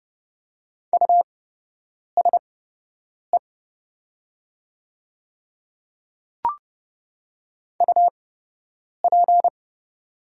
Morse code, heard as a keyed tone, about 29 wpm, spaced as for 8 wpm, 700 Hz. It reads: USE UP
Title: 8x Character Spacing - Top 100 Words - 30wpm